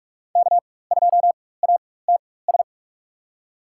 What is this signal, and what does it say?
Morse code, heard as a keyed tone, about 46 wpm, spaced as for 21 wpm, 700 Hz. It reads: K2ATS
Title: USA Call Signs - 5 Characters - 4x Character Spacing - 45wpm